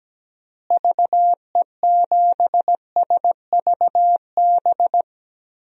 Morse code, heard as a keyed tone, about 17 wpm, 700 Hz. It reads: VE7SVB
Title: DXCC-100 Call Signs - 6 Characters - Repeated 2x - 17wpm